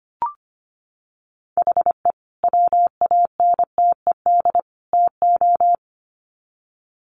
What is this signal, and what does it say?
Morse code, heard as a keyed tone, about 25 wpm, 700 Hz. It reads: HE WANTED TO
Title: Sets of 3 Words 25wpm